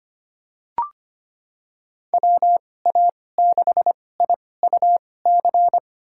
Morse code, heard as a keyed tone, about 25 wpm, spaced as for 18 wpm, 700 Hz. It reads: WA6IUC
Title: USA Call Signs - 6 Characters - 2x Character Spacing - 25wpm